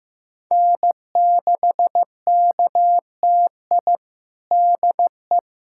Morse code, heard as a keyed tone, about 15 wpm, 700 Hz. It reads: N6KTI DE
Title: USA Callsign DE Callsign - 5 Characters - 15wpm